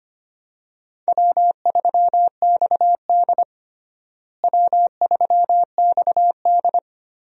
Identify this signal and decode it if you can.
Morse code, heard as a keyed tone, about 25 wpm, 700 Hz. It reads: W3XD W3XD
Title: USA Call Signs - 4 Characters - Repeated 3x - 25wpm